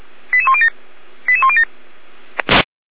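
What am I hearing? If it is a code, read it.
Five-tone selective call (ZVEI), heard as quick runs of five tones, tones of about 70 ms. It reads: 88107 88107
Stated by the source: ZVEI five-tone call received by SDR, fire brigade feuerwehr-orlamuende.org.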